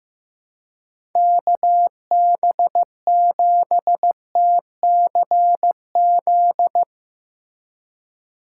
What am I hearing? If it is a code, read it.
Morse code, heard as a keyed tone, about 16 wpm, 700 Hz. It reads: KB7TCZ
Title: Speed Racing - USA 2x3 Call Signs 15wpm